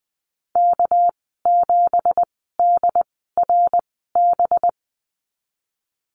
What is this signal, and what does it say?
Morse code, heard as a keyed tone, about 20 wpm, spaced as for 15 wpm, 700 Hz. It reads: K7DRB
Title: 231 Lesson 37g - Letter x - Call Signs 20wpm